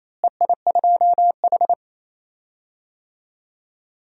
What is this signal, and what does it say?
Morse code, heard as a keyed tone, about 28 wpm, 700 Hz. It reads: EI2H
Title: DXCC-100 Call Signs - 4 Characters - Repeated 2x - 28wpm